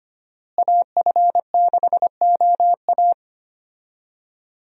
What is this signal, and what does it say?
Morse code, heard as a keyed tone, about 25 wpm, 700 Hz. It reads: AF6OA
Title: USA Call Signs - 5 Characters - Repeated 2x - 25wpm